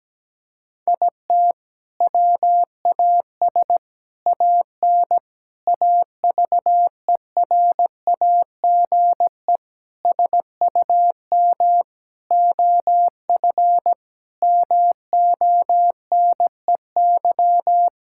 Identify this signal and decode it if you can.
Morse code, heard as a keyed tone, about 17 wpm, 700 Hz. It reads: IT WAS AN AVERAGE SUM OF MONEY
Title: Sentences from Top 1000 Words - Review Part 6 - 17wpm